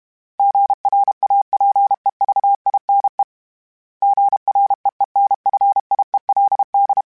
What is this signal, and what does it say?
Morse code, heard as a keyed tone, about 32 wpm, 800 Hz. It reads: GRAPEVINE GREENFIELD
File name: fromSMs / FromSMs mp3s / Cities@30